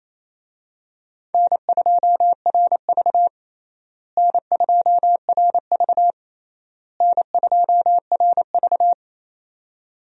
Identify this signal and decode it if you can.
Morse code, heard as a keyed tone, about 28 wpm, 700 Hz. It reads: N2RV N2RV N2RV
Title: USA Call Signs - 4 Characters - Repeated 3x - 28wpm